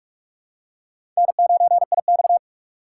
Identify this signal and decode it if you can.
Morse code, heard as a keyed tone, about 46 wpm, 700 Hz. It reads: N9IX
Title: USA Call Signs - 4 Characters - 6x Character Spacing - 45wpm